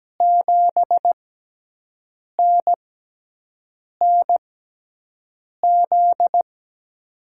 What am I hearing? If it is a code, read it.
Morse code, heard as a keyed tone, about 17 wpm, spaced as for 6 wpm, 700 Hz. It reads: 7NNZ
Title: USA Call Signs - 5 Characters - 6x Character Spacing - 17wpm